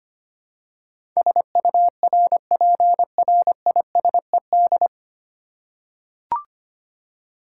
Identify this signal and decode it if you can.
Morse code, heard as a keyed tone, about 25 wpm, 700 Hz. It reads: SURPRISED
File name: Nine Letter Words - 25wpm